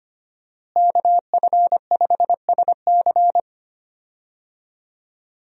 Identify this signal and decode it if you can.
Morse code, heard as a keyed tone, about 25 wpm, 700 Hz. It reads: KF5SC
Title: USA Call Signs - 5 Characters - Repeated 2x - 25wpm